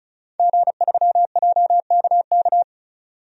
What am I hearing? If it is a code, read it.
Morse code, heard as a keyed tone, about 34 wpm, 700 Hz. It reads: G3JKK
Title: DXCC-100 Call Signs - 5 Characters - 35wpm